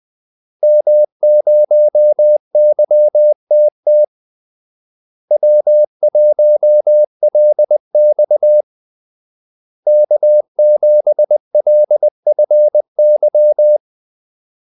Morse code, heard as a keyed tone, about 20 wpm, 600 Hz. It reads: M0YTT W1LX K7LFY
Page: Kaldesignaler 20 wpm